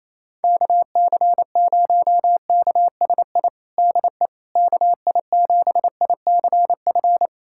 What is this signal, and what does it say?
Morse code, heard as a keyed tone, about 28 wpm, 700 Hz. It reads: KC0KSI DE KI7ICF
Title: USA Callsign DE Callsign - 6 Characters - 28wpm